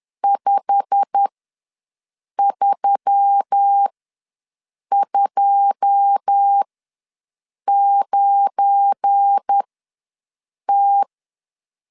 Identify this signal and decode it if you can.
Morse code, heard as a keyed tone, about 10 wpm, spaced as for 6 wpm, 800 Hz. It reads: 5329T
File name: LEZIONI / 9 2 ws 10wpm 4130